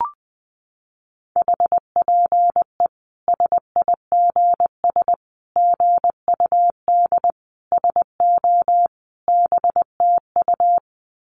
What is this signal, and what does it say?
Morse code, heard as a keyed tone, about 20 wpm, 700 Hz. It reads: HPE SIGS GUD SO BTU
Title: Common QSO Phrases 20wpm